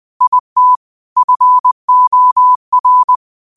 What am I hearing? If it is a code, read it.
Morse code, heard as a keyed tone, about 20 wpm, 1000 Hz. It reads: IT FOR